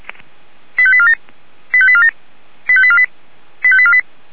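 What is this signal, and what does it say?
Five-tone selective call (ZVEI), heard as quick runs of five tones, tones of about 70 ms. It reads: 85847 85847 85848 85848
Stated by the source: Five-tone ZVEI selective call received on an SDR, fire brigade Feuerwehr Orlamünde.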